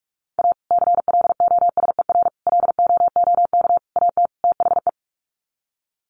Morse code, heard as a keyed tone, about 44 wpm, 700 Hz. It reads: A CLOSER LOOK AT THE
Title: Sets of 5 Words Encore 45wpm